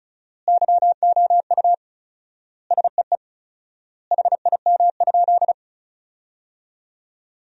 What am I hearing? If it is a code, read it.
Morse code, heard as a keyed tone, about 34 wpm, 700 Hz. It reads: YOU SEE HIM?
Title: 4x Word-Spacing - Sentences from Top 100 Words - 35wpm